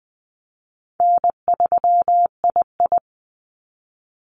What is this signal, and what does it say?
Morse code, heard as a keyed tone, about 20 wpm, 700 Hz. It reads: N3II
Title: NA SOTA Activator Callsigns 20wpm